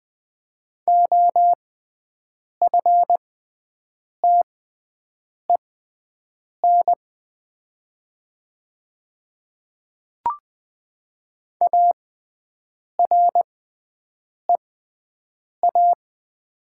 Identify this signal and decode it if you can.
Morse code, heard as a keyed tone, about 20 wpm, spaced as for 7 wpm, 700 Hz. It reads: OFTEN AREA
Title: 6x Character Spacing - Top 500 Words - 20wpm